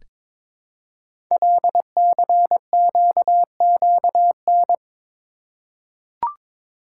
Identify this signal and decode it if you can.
Morse code, heard as a keyed tone, about 22 wpm, 700 Hz. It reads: LCQQN
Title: Code Groups - 5 Letters - 22wpm